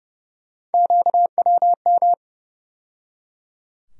Morse code, heard as a keyed tone, about 30 wpm, 700 Hz. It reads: QWM
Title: USA Call Sign Postfixes - 3 Letters 30wpm